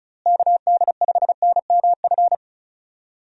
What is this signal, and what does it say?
Morse code, heard as a keyed tone, about 36 wpm, 700 Hz. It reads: KD5NMF
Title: USA Call Signs - 6 Characters - Repeated 3x - 35wpm